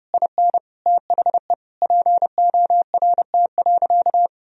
Morse code, heard as a keyed tone, about 30 wpm, 700 Hz. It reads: IN THE PORT.
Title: Sentences from Top 1000 Words - Review Part 1 - 30wpm